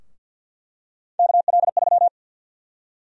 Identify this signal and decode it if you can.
Morse code, heard as a keyed tone, about 49 wpm, 700 Hz. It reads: KL3
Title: USA Call Sign Prefixes - 50wpm